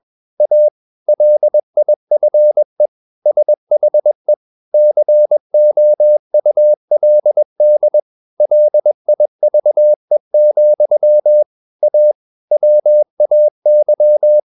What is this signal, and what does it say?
Morse code, heard as a keyed tone, about 21 wpm, 600 Hz. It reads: A LIFE SHE COULD LIVE, A WAY
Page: CW med Gnister